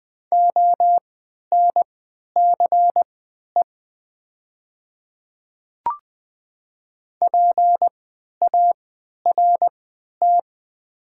Morse code, heard as a keyed tone, about 20 wpm, spaced as for 11 wpm, 700 Hz. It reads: ONCE PART
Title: 3x Character Spacing - Top 300 Words - 20wpm